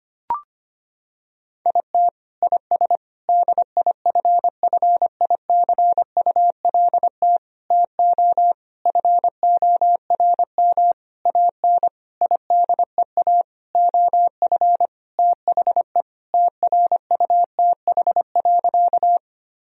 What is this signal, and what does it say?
Morse code, heard as a keyed tone, about 25 wpm, 700 Hz. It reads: IT IS DIFFICULT TO FORM AN IDEA OF THE TRUTH.
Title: Sentences from Top 1000 Words - Review Part 1 - 25wpm